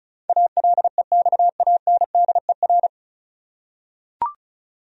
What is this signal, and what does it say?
Morse code, heard as a keyed tone, about 36 wpm, 700 Hz. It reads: ALEXANDER
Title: USA Common Names - 35wpm